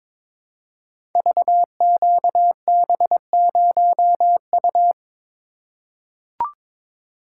Morse code, heard as a keyed tone, about 22 wpm, 700 Hz. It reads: VQB0U
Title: Code Groups - 5 Letter:Numbers - 22wpm